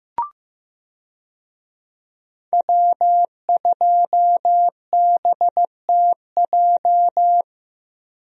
Morse code, heard as a keyed tone, about 15 wpm, 700 Hz. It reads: W2BTJ
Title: USA Call Signs - 5 Characters - Repeated 2x - 15wpm